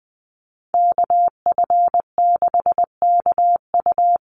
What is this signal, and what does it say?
Morse code, heard as a keyed tone, about 20 wpm, 700 Hz. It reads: KF6KU
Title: USA Call Signs 20wpm